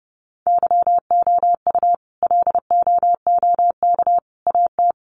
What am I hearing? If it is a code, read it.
Morse code, heard as a keyed tone, about 30 wpm, 700 Hz. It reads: YOU LOOK AT
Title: Sets of 3 Words 30wpm